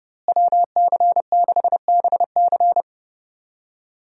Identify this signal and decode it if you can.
Morse code, heard as a keyed tone, about 30 wpm, 700 Hz. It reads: WC6BC
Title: USA Call Signs - 5 Characters - Repeated 3x - 30wpm